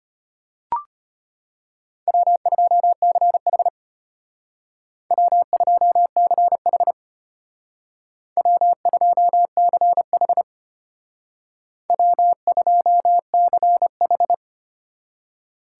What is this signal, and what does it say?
Morse code, heard as a keyed tone, about 30 wpm, 700 Hz. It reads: W2CH W2CH W2CH W2CH
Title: Speed Racing - USA 1x2 Call Signs 25wpm